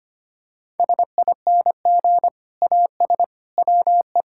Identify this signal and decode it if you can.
Morse code, heard as a keyed tone, about 25 wpm, 700 Hz. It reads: SING AS WE
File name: practice / 25wpm / Sentences with 2-Word Phrases - 25wpm